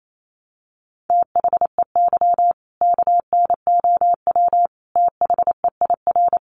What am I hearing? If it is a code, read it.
Morse code, heard as a keyed tone, about 28 wpm, 700 Hz. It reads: THEY KNOW THEIR
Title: Sentences from Top 1200 Words 28wpm